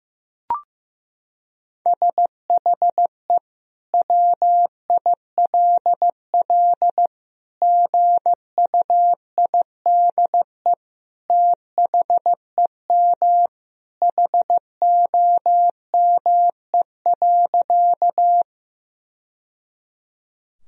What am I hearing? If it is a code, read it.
Morse code, heard as a keyed tone, about 15 wpm, 700 Hz. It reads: SHE WILL GUIDE THEM HOME.
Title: Sentences from Top 1000 Words - Review Part 3 - 15wpm